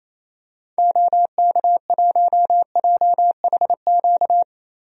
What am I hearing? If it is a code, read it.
Morse code, heard as a keyed tone, about 28 wpm, 700 Hz. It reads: OK1JHQ
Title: DXCC-100 Call Signs - 6 Characters - Repeated 2x - 28wpm